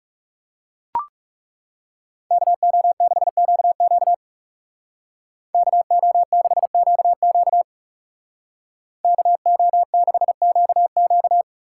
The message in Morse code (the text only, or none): KO6QQ KO6QQ KO6QQ